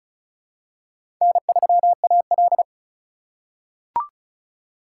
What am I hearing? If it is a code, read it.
Morse code, heard as a keyed tone, about 36 wpm, 700 Hz. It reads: N3AL
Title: USA Call Signs - 4x Character Spacing - 35wpm